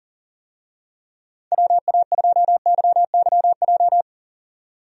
Morse code, heard as a keyed tone, about 40 wpm, 700 Hz. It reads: WA2YYJ
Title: USA Call Signs - 6 Characters - 40wpm